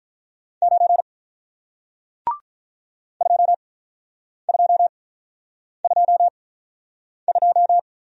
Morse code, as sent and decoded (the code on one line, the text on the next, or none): ----. / ..--- ..--- ..--- ..---
9 2222